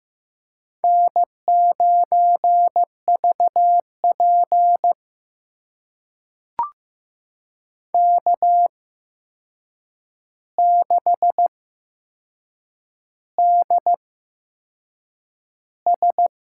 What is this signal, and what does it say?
Morse code, heard as a keyed tone, about 15 wpm, spaced as for 6 wpm, 700 Hz. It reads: N9VP K6DS
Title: USA Call Signs - 4 Characters - 8x Character Spacing - 15wpm